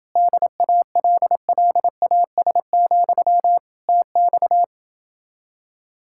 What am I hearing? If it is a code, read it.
Morse code, heard as a keyed tone, about 29 wpm, 700 Hz. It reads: DALLAS, TX